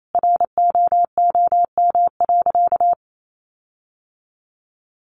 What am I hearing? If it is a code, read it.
Morse code, heard as a keyed tone, about 28 wpm, 700 Hz. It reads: ROOM.
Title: Sentences from Top 1300 Words 28wpm